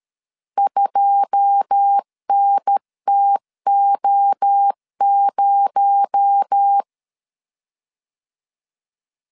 Morse code, heard as a keyed tone, about 13 wpm, 800 Hz. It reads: 2NTO0
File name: LEZIONI / 10 SIGNS / 10B INT 12wpm 2414